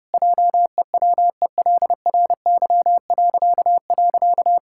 Morse code, heard as a keyed tone, about 32 wpm, 700 Hz. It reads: JEWELRY..
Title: Pangram Challenge v2 - 30wpm